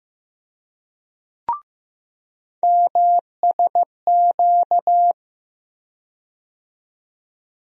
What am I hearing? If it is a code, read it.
Morse code, heard as a keyed tone, about 15 wpm, 700 Hz. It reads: MSQ
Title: USA Call Sign Postfixes - 3 Letters 15wpm